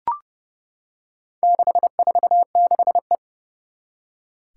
Morse code, heard as a keyed tone, about 32 wpm, 700 Hz. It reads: 646E
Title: Four Challenging Letter-Number Combinations 30wpm